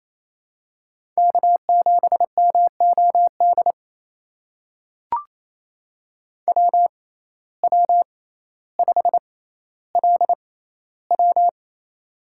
Morse code, heard as a keyed tone, about 28 wpm, spaced as for 14 wpm, 700 Hz. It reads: K7MOD WW5LW
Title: USA Call Signs - 6x Character Spacing - 28wpm